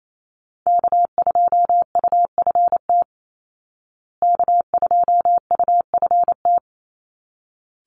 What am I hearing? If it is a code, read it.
Morse code, heard as a keyed tone, about 28 wpm, 700 Hz. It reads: K2UFT K2UFT
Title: Call Signs Repeated 3x - 28wpm